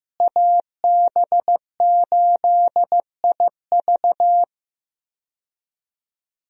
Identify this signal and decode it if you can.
Morse code, heard as a keyed tone, about 15 wpm, 700 Hz. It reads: AB8IV
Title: USA Call Signs - 5 Characters - Repeated 3x - 15wpm